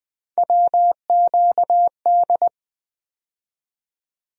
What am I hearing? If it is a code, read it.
Morse code, heard as a keyed tone, about 21 wpm, 700 Hz. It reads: WQD
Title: USA Call Sign Postfixes - 3 Letters 20wpm